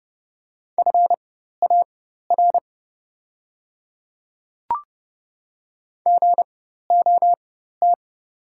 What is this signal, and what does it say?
Morse code, heard as a keyed tone, about 31 wpm, spaced as for 14 wpm, 700 Hz. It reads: FAR GOT
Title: 4x Character Spacing - Top 300 Words - 30wpm